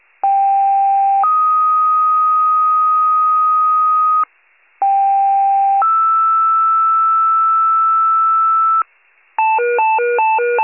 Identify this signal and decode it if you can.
Two-tone page, heard as a one-second tone, then a three-second tone, about 790 then 1250 Hz, about 790 then 1320 Hz. It is heard from 0.2 to 8.8 s.
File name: 83 FIRE